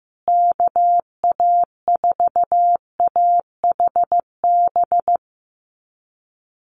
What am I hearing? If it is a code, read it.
Morse code, heard as a keyed tone, about 15 wpm, 700 Hz. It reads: KA4AHB
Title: USA Call Signs Repeated 2x 15wpm